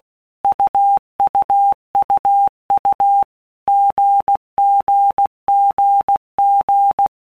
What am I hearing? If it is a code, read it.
Morse code, heard as a keyed tone, about 16 wpm, 800 Hz. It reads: UUUU GGGG
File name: All Groups 1-minute Blocks / Groups@15 / AWUNDG